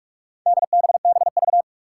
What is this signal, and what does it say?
Morse code, heard as a keyed tone, about 42 wpm, 700 Hz. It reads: DBBV
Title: Four Challenging Letter-Number Combinations 45wpm